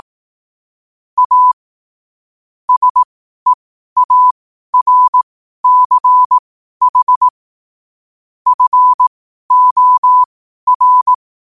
Morse code, heard as a keyed tone, about 18 wpm, spaced as for 12 wpm, 1000 Hz. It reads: A SEARCH FOR
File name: qotdfiles / 12wpm